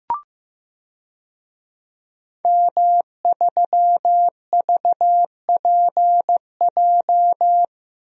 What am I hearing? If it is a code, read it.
Morse code, heard as a keyed tone, about 16 wpm, 700 Hz. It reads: M3VPJ
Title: DXCC-100 Call Signs - 5 Characters - Repeated 2x - 15wpm